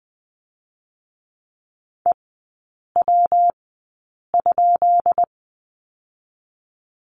Morse code, heard as a keyed tone, about 20 wpm, spaced as for 8 wpm, 700 Hz. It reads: EW?
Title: 205 Lesson 33e - Letter j - Three Characters 20wpm